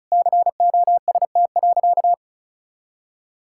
Morse code, heard as a keyed tone, about 36 wpm, 700 Hz. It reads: COST.
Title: Sentences from Top 1000 Words - Review Part 5 - 35wpm